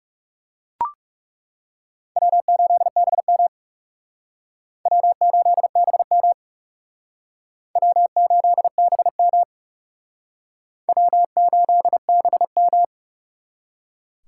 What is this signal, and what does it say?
Morse code, heard as a keyed tone, about 37 wpm, 700 Hz. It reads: W8BM W8BM W8BM W8BM
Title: Speed Racing - USA 1x2 Call Signs 30wpm